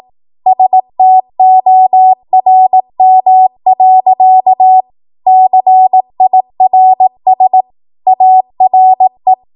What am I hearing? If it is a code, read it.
Morse code, heard as a keyed tone, about 18 wpm, 750 Hz. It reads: STORM. CIRS ARE
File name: Morse / 18WPM